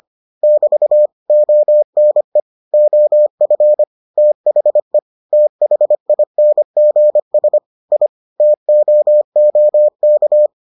Morse code, heard as a keyed tone, about 25 wpm, 600 Hz. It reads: <BT> ONE OF THE THINGS I TOOK